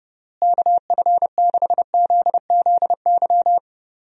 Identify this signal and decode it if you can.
Morse code, heard as a keyed tone, about 30 wpm, 700 Hz. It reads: KF6ZZY
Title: USA Call Signs - 6 Characters - Repeated 3x - 30wpm